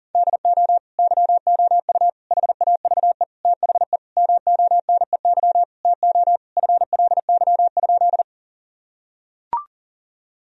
Morse code, heard as a keyed tone, about 40 wpm, 700 Hz. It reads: DO YOU HAVE THE MONEY TO FLY?
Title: Sentences from Top 1000 Words - Review Part 2 - 40wpm